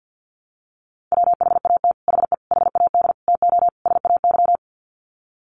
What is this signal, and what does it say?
Morse code, heard as a keyed tone, about 50 wpm, 700 Hz. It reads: WHAT HE HAD TO SAY
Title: Sets of 5 Words Encore2 50wpm